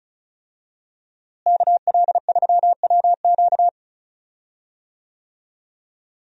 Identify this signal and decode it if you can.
Morse code, heard as a keyed tone, about 34 wpm, 700 Hz. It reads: KL3WQ
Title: USA Call Signs - 5 Characters - Repeated 3x - 35wpm